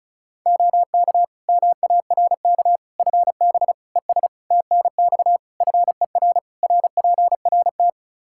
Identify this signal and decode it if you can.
Morse code, heard as a keyed tone, about 35 wpm, 700 Hz. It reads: OK MARK FB ES TNX FER RPRT
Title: Practice QSO - Exchange 2 - Info - 35wpm